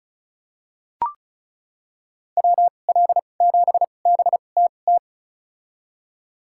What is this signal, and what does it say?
Morse code, heard as a keyed tone, about 34 wpm, spaced as for 25 wpm, 700 Hz. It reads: WL7BTT
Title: USA Call Signs - 6 Characters - 2x Character Spacing - 35wpm